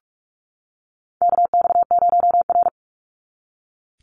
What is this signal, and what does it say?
Morse code, heard as a keyed tone, about 46 wpm, 700 Hz. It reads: KX0R
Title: NA SOTA Chaser Callsigns 45wpm